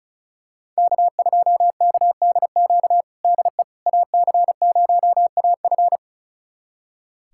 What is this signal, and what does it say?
Morse code, heard as a keyed tone, about 36 wpm, 700 Hz. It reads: K2KDQ DE AC0AF
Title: USA Callsign DE Callsign - 5 Characters - 35wpm